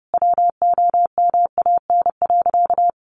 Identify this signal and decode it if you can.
Morse code, heard as a keyed tone, about 30 wpm, 700 Hz. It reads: WOMAN.
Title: Sentences from Top 1400 Words 30wpm